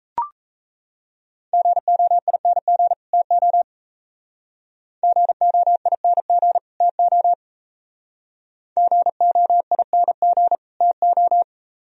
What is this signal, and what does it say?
Morse code, heard as a keyed tone, about 36 wpm, 700 Hz. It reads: GOING TO GOING TO GOING TO